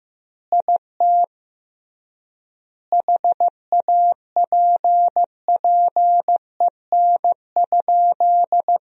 Morse code, heard as a keyed tone, about 15 wpm, 700 Hz. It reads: IT HAPPEN?
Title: 3x Word-Spacing - Sentences from Top 400 Words - 15wpm